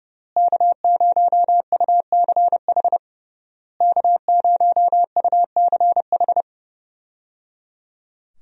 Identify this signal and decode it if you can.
Morse code, heard as a keyed tone, about 30 wpm, 700 Hz. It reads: K0UCH K0UCH
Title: USA Call Signs - 5 Characters - Repeated 3x - 30wpm